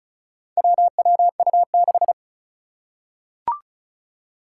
Morse code, heard as a keyed tone, about 36 wpm, 700 Hz. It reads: WWU6
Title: Four Challenging Letter-Number Combinations 35wpm